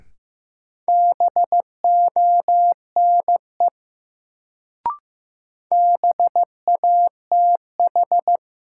Morse code, heard as a keyed tone, about 15 wpm, 700 Hz. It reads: BONE BATH
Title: Sentences from Top 1600 Words - New Word Drill Alt - 15wpm